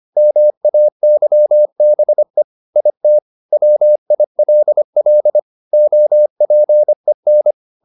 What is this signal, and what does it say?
Morse code, heard as a keyed tone, about 25 wpm, 600 Hz. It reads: MAYBE IT WILL OPEN
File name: the old man and the sea part 1 25wpm